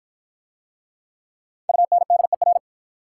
Morse code, heard as a keyed tone, about 54 wpm, 690 Hz. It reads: UNDER